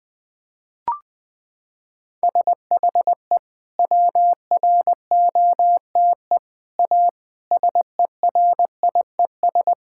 Morse code, heard as a keyed tone, about 20 wpm, 700 Hz. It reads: SHE WROTE A SERIES
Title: Sentences with 3-Word Phrases - 20wpm